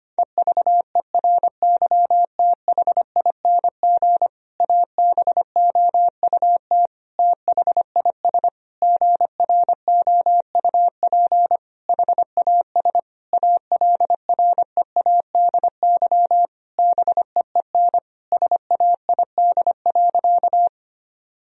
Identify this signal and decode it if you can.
Morse code, heard as a keyed tone, about 25 wpm, 700 Hz. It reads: EVERYTHING ABOUT THIS GROUP HAS ALREADY BEEN SAID.
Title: Sentences from Top 1000 Words - Review Part 1 - 25wpm